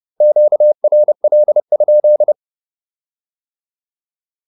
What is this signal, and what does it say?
Morse code, heard as a keyed tone, about 30 wpm, 600 Hz. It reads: QRL?